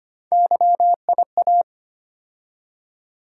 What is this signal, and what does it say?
Morse code, heard as a keyed tone, about 25 wpm, 700 Hz. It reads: YIA
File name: Three Letter-Number Combinations - 25wpm